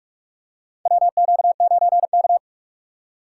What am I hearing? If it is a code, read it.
Morse code, heard as a keyed tone, about 45 wpm, 700 Hz. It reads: WQ9K
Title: USA Call Signs - 4 Characters - 6x Character Spacing - 45wpm